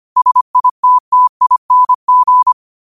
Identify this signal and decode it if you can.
Morse code, heard as a keyed tone, about 25 wpm, 1000 Hz. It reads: SITTING
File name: qotdfiles / 25wpm